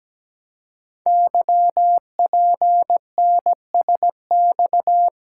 Morse code, heard as a keyed tone, about 17 wpm, 700 Hz. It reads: YPNSX
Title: Code Groups - 5 Letter:Numbers - 17wpm